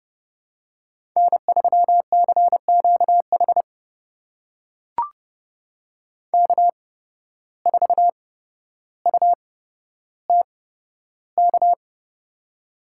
Morse code, heard as a keyed tone, about 28 wpm, spaced as for 13 wpm, 700 Hz. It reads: N3CQH K4UTK
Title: USA Call Signs - 5 Characters - 8x Character Spacing - 30wpm